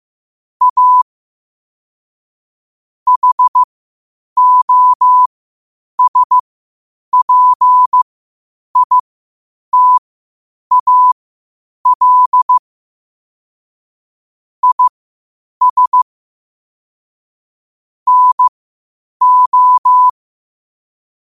Morse code, heard as a keyed tone, about 15 wpm, spaced as for 8 wpm, 1000 Hz. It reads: A HOSPITAL IS NO
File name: qotdfiles / 8wpm